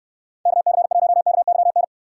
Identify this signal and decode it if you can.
Morse code, heard as a keyed tone, about 69 wpm, 700 Hz. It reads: KC1KYN